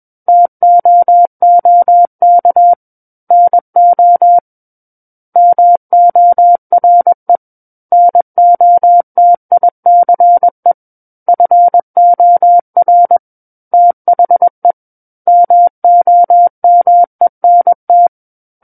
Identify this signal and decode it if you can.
Morse code, heard as a keyed tone, about 21 wpm, 700 Hz. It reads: TOOK NO MORE NOTICE FOR THE MOMENT
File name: Book One / 12-Chapter 12 - 21 WPM